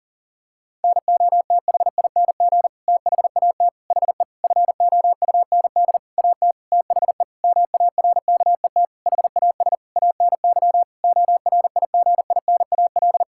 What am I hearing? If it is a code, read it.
Morse code, heard as a keyed tone, about 40 wpm, 700 Hz. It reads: NOTHING THAT HE FOUND AT THE MARKET HAS ANY ORIGINAL
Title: Sentences from Top 1000 Words - Review Part 6 - 40wpm